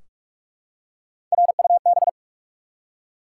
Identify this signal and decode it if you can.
Morse code, heard as a keyed tone, about 44 wpm, 700 Hz. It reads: RUB
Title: Three Letter Words - 45wpm